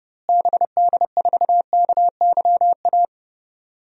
Morse code, heard as a keyed tone, about 30 wpm, 700 Hz. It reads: BD4KYA
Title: DXCC-100 Call Signs - 6 Characters - Repeated 2x - 30wpm